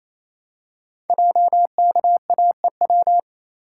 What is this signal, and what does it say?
Morse code, heard as a keyed tone, about 29 wpm, 700 Hz. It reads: JKAEW
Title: Code Groups - 5 Letters - 28wpm